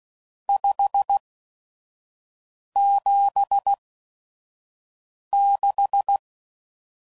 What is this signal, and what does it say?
Morse code, heard as a keyed tone, about 16 wpm, spaced as for 5 wpm, 800 Hz. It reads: 576
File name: CW Code Groups @5WPM / RST's / Various RST No's